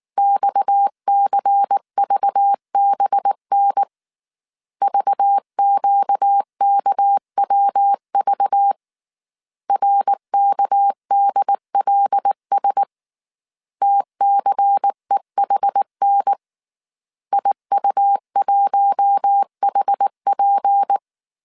Morse code, heard as a keyed tone, about 19 wpm, 800 Hz. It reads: XCVBN VQKWV RKDLS TCEHN IU1HP
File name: CW / DET / 1 6 1500C S A 18wpm 2255